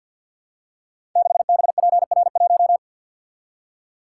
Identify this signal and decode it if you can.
Morse code, heard as a keyed tone, about 50 wpm, 700 Hz. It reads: 6BPR1
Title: Code Groups - 5 Letter:Numbers - 50wpm